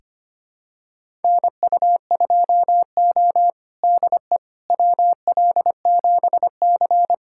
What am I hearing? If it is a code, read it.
Morse code, heard as a keyed tone, about 25 wpm, 700 Hz. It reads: NU2O DE WL7C
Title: USA Callsign DE Callsign - 4 Characters - 25wpm